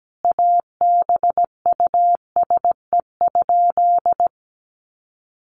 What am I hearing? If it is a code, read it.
Morse code, heard as a keyed tone, about 17 wpm, 700 Hz. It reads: ABUSE?
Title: Sentences from Top 1300 Words 17wpm